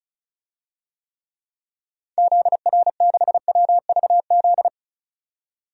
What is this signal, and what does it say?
Morse code, heard as a keyed tone, about 36 wpm, 700 Hz. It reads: ZR6WVZ
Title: DXCC-100 Call Signs - 6 Characters - 35wpm